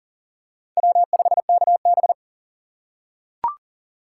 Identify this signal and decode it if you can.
Morse code, heard as a keyed tone, about 40 wpm, 700 Hz. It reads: W5KB
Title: USA Call Signs - 4x Character Spacing - 40wpm